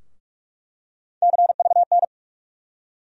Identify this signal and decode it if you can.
Morse code, heard as a keyed tone, about 48 wpm, 700 Hz. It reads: CVN
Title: USA Call Sign Postfixes - 3 Letters 45wpm